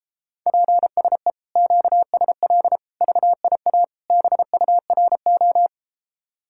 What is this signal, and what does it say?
Morse code, heard as a keyed tone, about 33 wpm, 690 Hz. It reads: PSE QSL VIA BURO